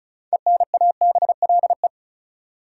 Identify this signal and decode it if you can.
Morse code, heard as a keyed tone, about 36 wpm, 700 Hz. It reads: ENABLE
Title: Sentences from Top 1500 Words - New Word Drill - 35wpm